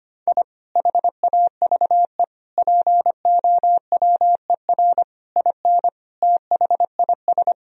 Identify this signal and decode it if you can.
Morse code, heard as a keyed tone, about 25 wpm, 700 Hz. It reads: I HAVE POWER IN THIS
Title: Sentences from Top 1000 Words - Review Part 2 - 25wpm